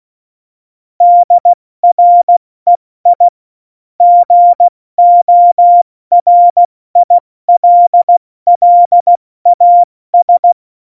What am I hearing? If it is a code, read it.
Morse code, heard as a keyed tone, about 16 wpm, 700 Hz. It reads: DREI GORILLAS
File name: Morsekurs / Witze / Witz 29. Dezember 2020, 70 F80